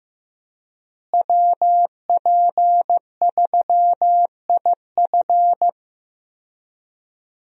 Morse code, heard as a keyed tone, about 15 wpm, 700 Hz. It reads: WP3IF
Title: USA Call Signs - 5 Characters - Repeated 2x - 15wpm